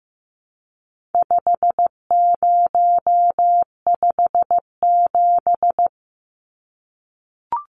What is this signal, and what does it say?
Morse code, heard as a keyed tone, about 15 wpm, 700 Hz. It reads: 5057
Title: Numbers - 4-Digit 15wpm